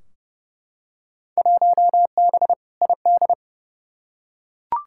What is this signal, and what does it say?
Morse code, heard as a keyed tone, about 32 wpm, 700 Hz. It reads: 1B ID
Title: Field Day Exchanges - 30wpm